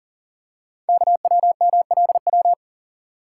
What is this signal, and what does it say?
Morse code, heard as a keyed tone, about 39 wpm, 700 Hz. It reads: KWMLW
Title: Code Groups - 5 Letters - 40wpm